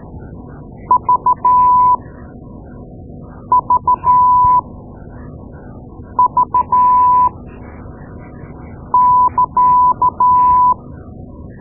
Morse code, heard as a keyed tone, about 13 wpm, spaced as for 5 wpm, 1000 Hz. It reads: VVV <KA>
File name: Morse Lessons / Morse Lessons / Tape-2A